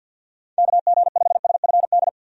Morse code, heard as a keyed tone, about 46 wpm, 700 Hz. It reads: KG5SFD
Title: USA Call Signs - 6 Characters - 50wpm